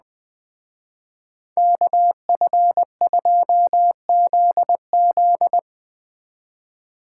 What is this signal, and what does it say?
Morse code, heard as a keyed tone, about 21 wpm, 700 Hz. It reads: KF2ZZ